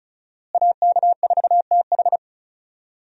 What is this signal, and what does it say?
Morse code, heard as a keyed tone, about 36 wpm, 700 Hz. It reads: AK4TH
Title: USA Call Signs - 5 Characters - Repeated 3x - 35wpm